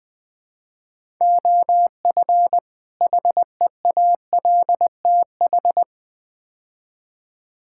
Morse code, heard as a keyed tone, about 20 wpm, 700 Hz. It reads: OF HEALTH
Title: Build A Phrase - 5-Word Phrases - 20wpm